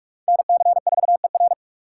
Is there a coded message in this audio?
Yes: Morse code from 0.3 to 1.5 s, heard as a keyed tone, about 42 wpm, 700 Hz. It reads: NC4ER